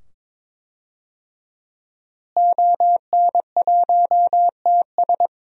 Morse code, heard as a keyed tone, about 22 wpm, 700 Hz. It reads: ON1TS
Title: DXCC-100 Call Signs - 5 Characters - Repeated 2x - 22wpm